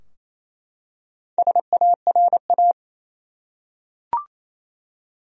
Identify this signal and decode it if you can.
Morse code, heard as a keyed tone, about 28 wpm, 700 Hz. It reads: SARA
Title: USA Common Names - Max 5 Letters - 28wpm